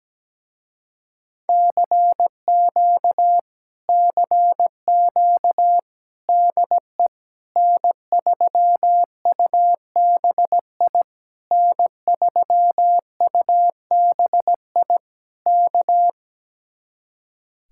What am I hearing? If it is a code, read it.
Morse code, heard as a keyed tone, about 17 wpm, 700 Hz. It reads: CQ CQ DE N3UBI N3UBI K
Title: Speed-Racing Common QSO Phrases - 17wpm